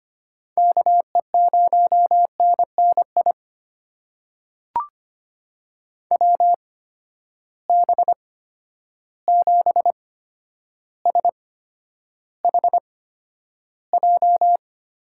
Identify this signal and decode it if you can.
Morse code, heard as a keyed tone, about 25 wpm, spaced as for 11 wpm, 700 Hz. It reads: KE0NNI WB7SHJ
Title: USA Call Signs - 8x Character Spacing - 25wpm